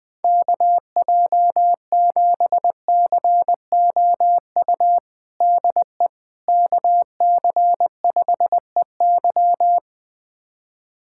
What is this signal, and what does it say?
Morse code, heard as a keyed tone, about 20 wpm, 700 Hz. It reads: KJ7COU DE KC5EY
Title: USA Callsign DE Callsign 20wpm